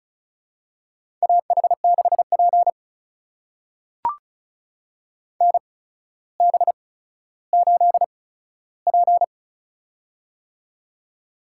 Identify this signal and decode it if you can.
Morse code, heard as a keyed tone, about 36 wpm, spaced as for 15 wpm, 700 Hz. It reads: AH6P NB8P
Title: USA Call Signs - 4 Characters - 8x Character Spacing - 35wpm